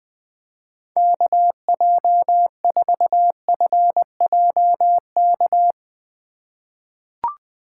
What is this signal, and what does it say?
Morse code, heard as a keyed tone, about 20 wpm, 700 Hz. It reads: KJ4FJK
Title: USA Call Signs - 6 Characters - 4x Character Spacing - 20wpm